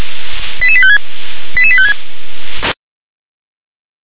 Five-tone selective call (ZVEI), heard as quick runs of five tones, tones of about 70 ms. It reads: 88856 88856
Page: SDR/ZVEI